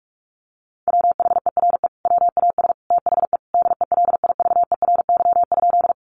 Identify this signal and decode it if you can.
Morse code, heard as a keyed tone, about 44 wpm, 700 Hz. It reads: WHERE WAS THE DELIVERY?
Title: Sentences from Top 1500 Words 45wpm